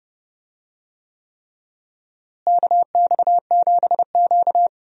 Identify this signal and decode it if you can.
Morse code, heard as a keyed tone, about 31 wpm, 700 Hz. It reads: KX7Q
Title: USA Call Signs - 4 Characters - Repeated 3x - 30wpm